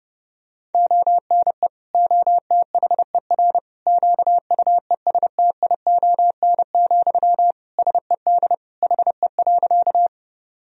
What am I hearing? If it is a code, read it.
Morse code, heard as a keyed tone, about 30 wpm, 700 Hz. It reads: ONE OTHER QUESTION, SED HE.
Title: Sentences from Top 400 Words and Abbreviations 30wpm